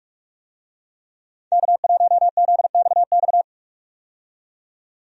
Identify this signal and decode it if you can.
Morse code, heard as a keyed tone, about 45 wpm, 700 Hz. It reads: K1ZXX
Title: USA Call Signs - 5 Characters - 45wpm